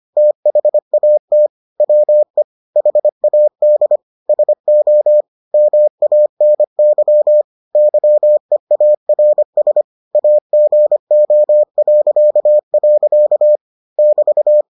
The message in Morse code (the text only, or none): THAT WE HAD SO MANY YEARS AGO.. <BT>